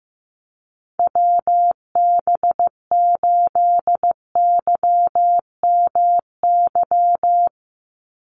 Morse code, heard as a keyed tone, about 15 wpm, 700 Hz. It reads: WB8YMY
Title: USA Call Signs 15wpm